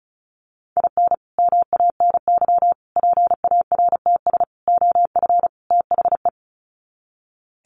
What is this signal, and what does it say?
Morse code, heard as a keyed tone, about 34 wpm, 700 Hz. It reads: IN MANY PARTS OF THE
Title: Sets of 5 Words Encore 35wpm